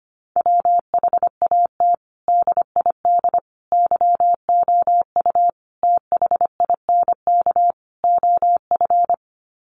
Morse code, heard as a keyed tone, about 25 wpm, 700 Hz. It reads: WHAT DID YOU THINK OF
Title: Sets of 5 Words Encore2 25wpm